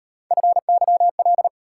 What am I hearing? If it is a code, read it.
Morse code, heard as a keyed tone, about 38 wpm, 700 Hz. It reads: FYL